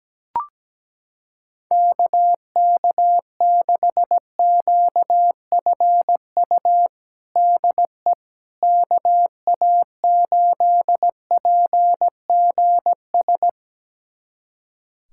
Morse code, heard as a keyed tone, about 17 wpm, 700 Hz. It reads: KK6QFU DE KA8PGS
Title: USA Callsign DE Callsign - 6 Characters - 17wpm